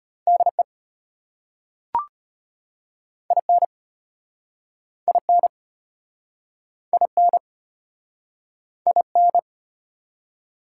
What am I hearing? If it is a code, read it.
Morse code, heard as a keyed tone, about 34 wpm, 700 Hz. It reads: DE IN IN IN IN